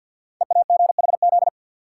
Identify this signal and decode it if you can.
Morse code, heard as a keyed tone, about 53 wpm, 700 Hz. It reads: EAGHZ
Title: Code Groups - 5 Letters - 50wpm